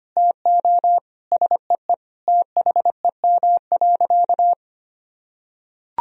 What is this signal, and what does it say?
Morse code, heard as a keyed tone, about 25 wpm, 700 Hz. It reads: TO SEE THEM.
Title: Sentences from Top 1000 Words - Review Part 1 - 25wpm